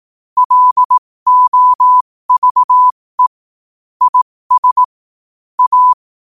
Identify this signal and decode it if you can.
Morse code, heard as a keyed tone, about 18 wpm, spaced as for 15 wpm, 1000 Hz. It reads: LOVE IS A
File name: qotdfiles / 15wpm